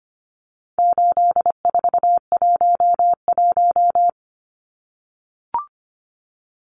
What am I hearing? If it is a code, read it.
Morse code, heard as a keyed tone, about 25 wpm, 700 Hz. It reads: 8411
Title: Numbers - 4-Digit 25wpm